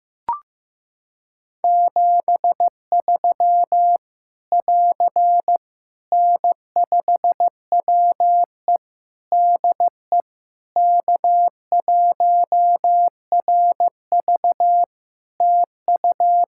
Morse code, heard as a keyed tone, about 16 wpm, 700 Hz. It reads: 73 <AR> N5WE DE K1RV TU
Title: Practice QSO - Exchange 4 - Ending - 15wpm